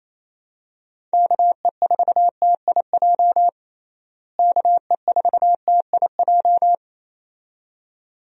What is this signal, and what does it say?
Morse code, heard as a keyed tone, about 27 wpm, 700 Hz. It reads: KE4TIJ KE4TIJ
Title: USA Call Signs - 6 Characters - Repeated 2x - 28wpm